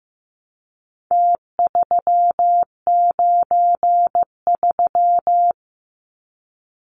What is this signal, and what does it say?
Morse code, heard as a keyed tone, about 15 wpm, 700 Hz. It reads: T393
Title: Numbers - Contest Style - 4-Digit 15wpm